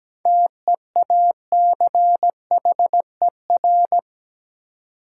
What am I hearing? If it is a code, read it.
Morse code, heard as a keyed tone, about 17 wpm, 700 Hz. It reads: TEACHER
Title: Seven Letter Words - 17wpm